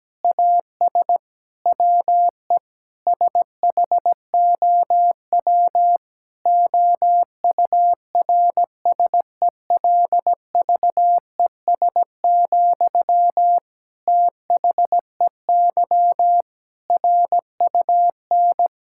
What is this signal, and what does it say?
Morse code, heard as a keyed tone, about 17 wpm, 700 Hz. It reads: AS WE SHOW OURSELVES, THEY RUN
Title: Sentences from Top 1600 Words 17wpm